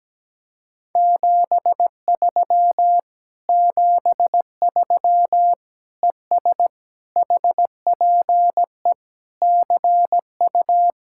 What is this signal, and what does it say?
Morse code, heard as a keyed tone, about 17 wpm, 700 Hz. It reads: 73 73 ES HPE CU
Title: Speed-Racing Common QSO Phrases - 17wpm